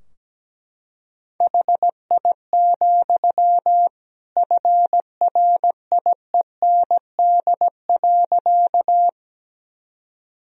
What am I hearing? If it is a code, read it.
Morse code, heard as a keyed tone, about 17 wpm, 700 Hz. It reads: HI, FRIEND.
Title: Sentences from Top 1000 Words - Review Part 3 - 17wpm